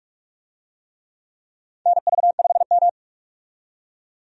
Morse code, heard as a keyed tone, about 46 wpm, 700 Hz. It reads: NV5M
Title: USA Call Signs - 4 Characters - 45wpm